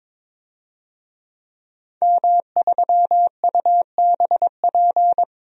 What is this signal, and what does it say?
Morse code, heard as a keyed tone, about 21 wpm, 700 Hz. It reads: M3UBP
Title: DXCC-100 Call Signs - 5 Characters - 22wpm